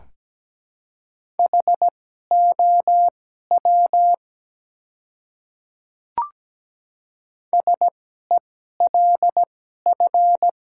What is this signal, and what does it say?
Morse code, heard as a keyed tone, about 17 wpm, spaced as for 12 wpm, 700 Hz. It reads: HOW SELF
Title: 2x Character Spacing - Top 200 Words - 17wpm